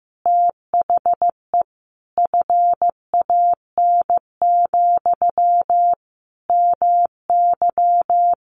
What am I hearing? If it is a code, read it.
Morse code, heard as a keyed tone, about 15 wpm, 700 Hz. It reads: THE FAN, MY
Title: Sentences from Top 1300 Words 15wpm